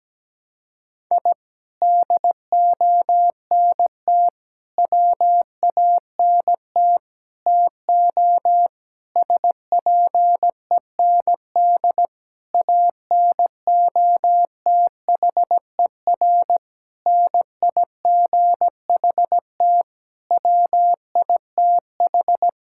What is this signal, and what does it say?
Morse code, heard as a keyed tone, about 17 wpm, 700 Hz. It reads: I DONT WANT TO SPEND ANOTHER NIGHT WITH
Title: Sentences from Top 1000 Words - Review Part 5 - 17wpm